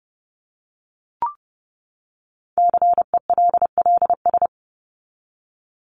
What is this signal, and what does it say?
Morse code, heard as a keyed tone, about 30 wpm, 700 Hz. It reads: CELLS
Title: ES:S Suffix - Top 100 Words with Base Words 30wpm